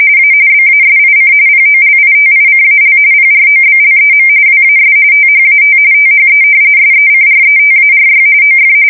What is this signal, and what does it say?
RTTY, about 45.45 baud. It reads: A BASIC POLITICAL PROBLEM WAS THAT THE SECRETARY OF
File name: fsk170 / usnavy5